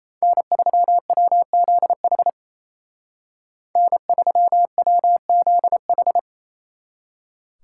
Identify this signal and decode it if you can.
Morse code, heard as a keyed tone, about 32 wpm, 700 Hz. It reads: N3WZH N3WZH
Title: Speed Racing - USA 1x3 Call Signs 28wpm